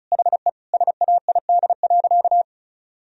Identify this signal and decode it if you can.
Morse code, heard as a keyed tone, about 34 wpm, 700 Hz. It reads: HE SAID.
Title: Sentences from Top 1000 Words - Review Part 2 - 35wpm